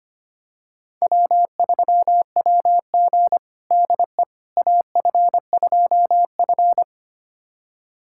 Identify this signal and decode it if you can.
Morse code, heard as a keyed tone, about 25 wpm, 700 Hz. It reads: W3WG DE AF2F
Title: USA Callsign DE Callsign - 4 Characters - 25wpm